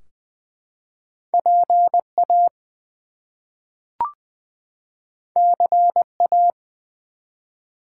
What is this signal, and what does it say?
Morse code, heard as a keyed tone, about 20 wpm, 700 Hz. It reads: PA CA
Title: US States and Canadian Provinces 20wpm